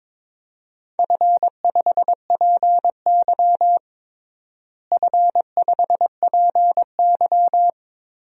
Morse code, heard as a keyed tone, about 22 wpm, 700 Hz. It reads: F5PY F5PY
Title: DXCC-100 Call Signs - 4 Characters - Repeated 2x - 22wpm